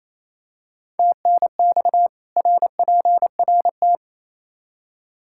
Morse code, heard as a keyed tone, about 28 wpm, 700 Hz. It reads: TNX RPRT
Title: Practice QSO - Exchange 2 - Info - 28wpm